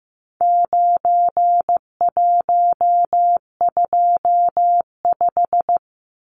Morse code, heard as a keyed tone, about 15 wpm, 700 Hz. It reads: 9125
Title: Numbers - 4-Digit 15wpm